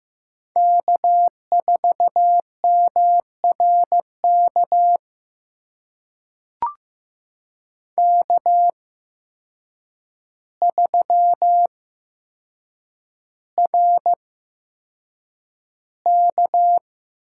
Morse code, heard as a keyed tone, about 15 wpm, spaced as for 7 wpm, 700 Hz. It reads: K4MRK K3RK
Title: USA Call Signs - 8x Character Spacing - 15wpm